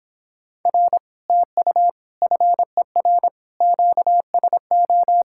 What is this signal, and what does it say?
Morse code, heard as a keyed tone, about 26 wpm, 700 Hz. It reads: R TU FER QSO